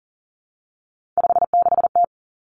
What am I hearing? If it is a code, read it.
Morse code, heard as a keyed tone, about 40 wpm, 700 Hz. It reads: 56T
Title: Three Challenging Letter-Number Combinations 40wpm